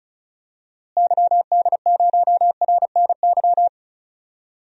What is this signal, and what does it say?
Morse code, heard as a keyed tone, about 34 wpm, 700 Hz. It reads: YD0RNY
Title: DXCC-100 Call Signs - 6 Characters - Repeated 3x - 35wpm